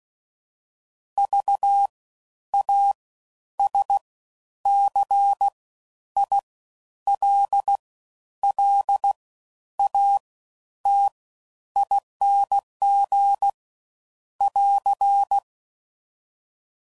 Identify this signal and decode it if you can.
Morse code, heard as a keyed tone, about 16 wpm, 800 Hz. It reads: V A S C I L L A T ING <AR>
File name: Suffixes / Special INGs / Fast Ings